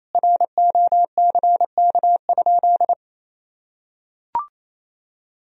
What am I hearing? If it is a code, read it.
Morse code, heard as a keyed tone, about 28 wpm, 700 Hz. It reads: ROCK?
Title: Sentences from Top 1000 Words - Review Part 2 - 28wpm